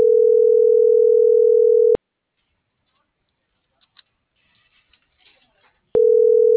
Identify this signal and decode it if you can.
Ringback tone (North America), 440 + 480 Hz.